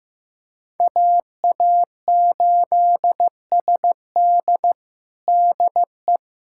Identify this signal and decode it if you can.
Morse code, heard as a keyed tone, about 16 wpm, 700 Hz. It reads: AA8SD DE
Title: USA Callsign DE Callsign - 5 Characters - 15wpm